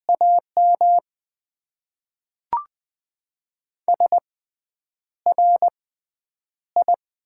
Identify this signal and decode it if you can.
Morse code, heard as a keyed tone, about 20 wpm, spaced as for 9 wpm, 700 Hz. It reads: AM SRI
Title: Top 90 QSO Elements - 6x Character Spacing - 20wpm